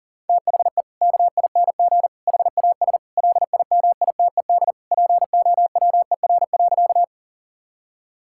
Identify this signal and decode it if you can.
Morse code, heard as a keyed tone, about 40 wpm, 700 Hz. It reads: THE KING HAS LIMITED POWER.
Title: Sentences from Top 1000 Words - Review Part 6 - 40wpm